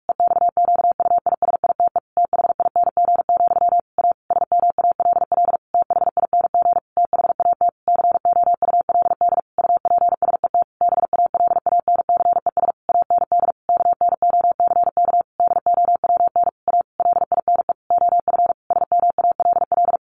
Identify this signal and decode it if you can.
Morse code, heard as a keyed tone, about 44 wpm, 700 Hz. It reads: EXQUISITE THING, A SMALL THING THAT COULD UPSET BALANCES AND KNOCK DOWN A LINE OF SMALL
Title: A Sound Of Thunder 45wpm